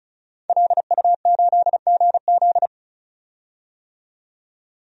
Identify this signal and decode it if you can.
Morse code, heard as a keyed tone, about 34 wpm, 700 Hz. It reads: LU8GZ
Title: DXCC-100 Call Signs - 5 Characters - 35wpm